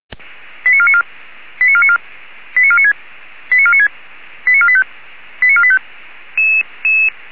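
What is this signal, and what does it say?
Five-tone selective call (ZVEI), heard as quick runs of five tones, tones of about 70 ms. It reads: 89494 89494 89496 89496 89485 89485
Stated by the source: ZVEI five-tone call 89155 received by SDR.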